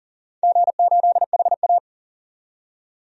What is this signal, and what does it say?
Morse code, heard as a keyed tone, about 42 wpm, 700 Hz. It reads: G8HA
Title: DXCC-100 Call Signs - 4 Characters - 40wpm